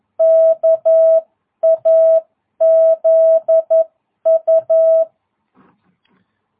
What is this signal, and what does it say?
Morse code, heard as a keyed tone, about 11 wpm, 660 Hz. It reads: KAZU
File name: jh8cbh / 2022cw